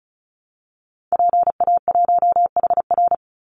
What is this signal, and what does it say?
Morse code, heard as a keyed tone, about 34 wpm, 700 Hz. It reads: PA1HR
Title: Call Signs Repeated 2x - 35wpm